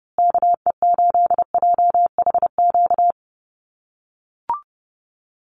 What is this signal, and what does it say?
Morse code, heard as a keyed tone, about 30 wpm, 700 Hz. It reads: KE8JHQ
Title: USA Call Signs Repeated 2x 30wpm